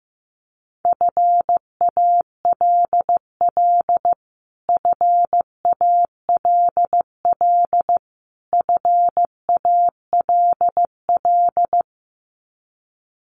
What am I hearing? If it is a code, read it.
Morse code, heard as a keyed tone, about 14 wpm, 700 Hz. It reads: FALL FALL FALL
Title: Top 500 Words Repeated 15wpm